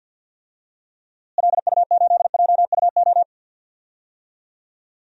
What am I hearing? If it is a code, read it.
Morse code, heard as a keyed tone, about 45 wpm, 700 Hz. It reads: LU8JUO
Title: DXCC-100 Call Signs - 6 Characters - Repeated 3x - 50wpm